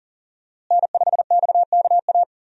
Morse code, heard as a keyed tone, about 40 wpm, 700 Hz. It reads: N5XKA
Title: USA Call Signs - 5 Characters - Repeated 2x - 40wpm